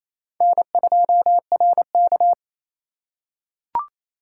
Morse code, heard as a keyed tone, about 27 wpm, 700 Hz. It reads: N2RK
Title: USA Call Signs - 4 Characters - 3x Character Spacing - 28wpm